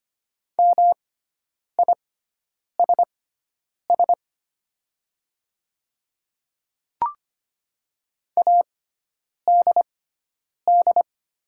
Morse code, heard as a keyed tone, about 25 wpm, spaced as for 9 wpm, 700 Hz. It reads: MISS ADD
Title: 6x Character Spacing - Top 500 Words - 25wpm